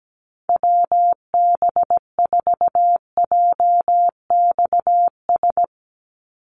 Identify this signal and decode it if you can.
Morse code, heard as a keyed tone, about 17 wpm, 700 Hz. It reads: WB4JXS
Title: USA Call Signs 17wpm